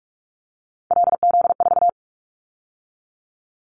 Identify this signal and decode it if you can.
Morse code, heard as a keyed tone, about 46 wpm, 700 Hz. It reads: LZ4
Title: Callsign Prefixes for Top 100 DXCC - 45wpm